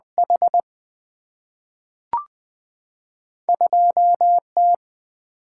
Morse code, heard as a keyed tone, about 20 wpm, 700 Hz. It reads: H 2 T
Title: Two Letter-Number Combinations - 20wpm